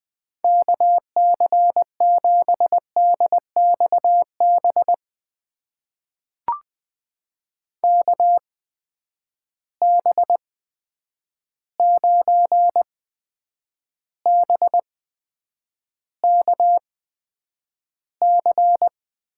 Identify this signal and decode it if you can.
Morse code, heard as a keyed tone, about 20 wpm, spaced as for 9 wpm, 700 Hz. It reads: KC7DXB KB9BKC
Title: USA Call Signs - 6 Characters - 8x Character Spacing - 20wpm